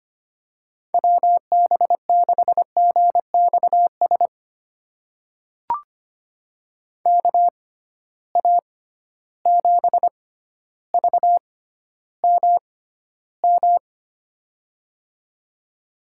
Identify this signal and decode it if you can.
Morse code, heard as a keyed tone, about 25 wpm, spaced as for 13 wpm, 700 Hz. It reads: WB6GXS KA7VMM
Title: USA Call Signs - 6 Characters - 6x Character Spacing - 25wpm